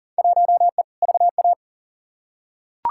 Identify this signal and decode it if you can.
Morse code, heard as a keyed tone, about 40 wpm, 700 Hz. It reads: 1E VA
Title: Field Day Exchanges - 40wpm